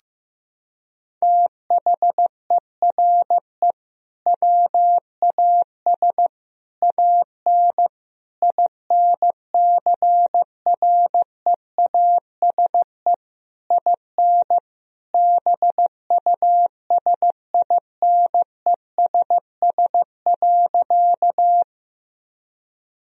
Morse code, heard as a keyed tone, about 15 wpm, 700 Hz. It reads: THERE WAS AN INCREASE IN BUSINESS.
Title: Sentences from Top 1000 Words - Review Part 4 - 15wpm